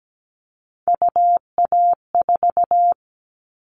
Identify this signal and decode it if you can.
Morse code, heard as a keyed tone, about 17 wpm, 700 Hz. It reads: UA4
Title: Callsign Prefixes for Top 100 DXCC - 17wpm